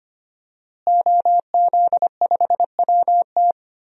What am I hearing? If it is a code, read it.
Morse code, heard as a keyed tone, about 25 wpm, 700 Hz. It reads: OZ5WT
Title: DXCC-100 Call Signs - 5 Characters - Repeated 3x - 25wpm